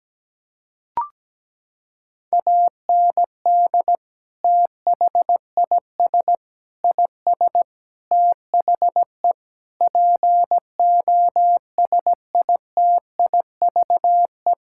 Morse code, heard as a keyed tone, about 17 wpm, 700 Hz. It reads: AND THIS IS THE POSITIVE
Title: Sentences from Top 1000 Words - Review Part 5 - 17wpm